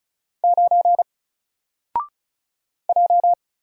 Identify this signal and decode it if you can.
Morse code, heard as a keyed tone, about 34 wpm, 700 Hz. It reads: 9 J
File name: Single-Letter-Number-35wpm